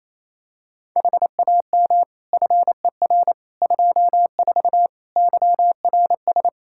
Morse code, heard as a keyed tone, about 28 wpm, 700 Hz. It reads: HAM FER 24 YRS
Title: Practice QSO - Exchange 3 - More Info - 28wpm